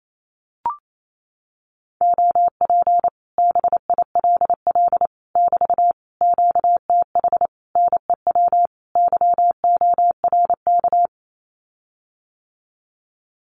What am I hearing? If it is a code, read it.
Morse code, heard as a keyed tone, about 28 wpm, 700 Hz. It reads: OP BILL <BT> QTH NEW YORK
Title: Common QSO Phrases Encore 28wpm